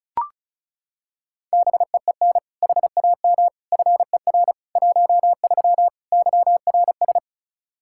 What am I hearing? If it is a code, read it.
Morse code, heard as a keyed tone, about 36 wpm, 700 Hz. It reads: BEEN HAM FER 13 YRS
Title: Practice QSO - Exchange 3 - More Info - 35wpm